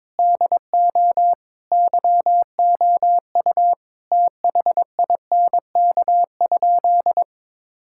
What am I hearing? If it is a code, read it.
Morse code, heard as a keyed tone, about 22 wpm, 700 Hz. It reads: DO YOU THINK?
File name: practice / 22wpm / Sentences with 2-Word Phrases - 22wpm